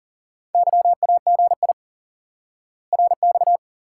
Morse code, heard as a keyed tone, about 37 wpm, 700 Hz. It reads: YAGI RX
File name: IQE Course 3 - Lesson 89 - YAGI - Two Elements - 40wpm